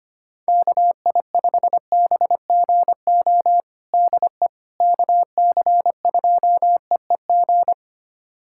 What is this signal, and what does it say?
Morse code, heard as a keyed tone, about 25 wpm, 700 Hz. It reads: KI5BGO DE KC2EEG
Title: USA Callsign DE Callsign 25wpm